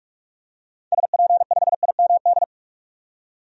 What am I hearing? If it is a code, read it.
Morse code, heard as a keyed tone, about 42 wpm, 700 Hz. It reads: SP5IMD
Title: DXCC-100 Call Signs - 6 Characters - 45wpm